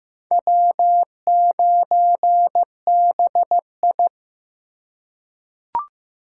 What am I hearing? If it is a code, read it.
Morse code, heard as a keyed tone, about 15 wpm, 700 Hz. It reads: W9BI
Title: USA Call Signs - 4 Characters - 2x Character Spacing - 15wpm